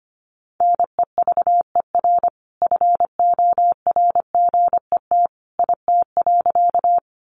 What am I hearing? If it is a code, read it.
Morse code, heard as a keyed tone, about 25 wpm, 700 Hz. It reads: NEVER FORGET IT.
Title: Sentences from Top 1200 Words 25wpm